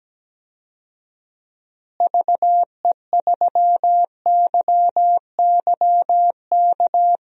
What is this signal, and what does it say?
Morse code, heard as a keyed tone, about 17 wpm, 700 Hz. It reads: VE3YYK
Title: DXCC-100 Call Signs - 6 Characters - Repeated 3x - 17wpm